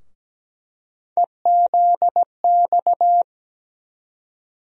Morse code, heard as a keyed tone, about 17 wpm, 700 Hz. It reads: EZX
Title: USA Call Sign Postfixes - 3 Letters 17wpm